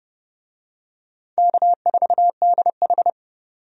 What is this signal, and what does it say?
Morse code, heard as a keyed tone, about 30 wpm, 700 Hz. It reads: K4DH
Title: USA Call Signs - 4 Characters - 30wpm